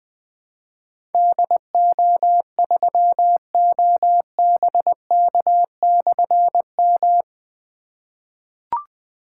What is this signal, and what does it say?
Morse code, heard as a keyed tone, about 20 wpm, 700 Hz. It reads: DO3OBK/M
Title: Difficult Call Signs 20wpm